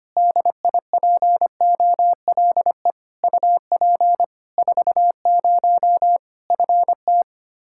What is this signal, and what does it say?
Morse code, heard as a keyed tone, about 25 wpm, 700 Hz. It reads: DIPOLE UP 40 FT
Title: Practice QSO - Exchange 2 - Info - 25wpm